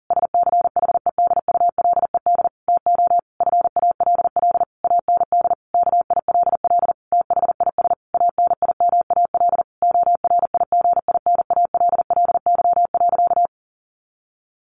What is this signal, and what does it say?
Morse code, heard as a keyed tone, about 37 wpm, 700 Hz. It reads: SCHEDULED TO FALL AND KILL THIS ANIMAL ORIGINALLY.
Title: A Sound Of Thunder 40wpm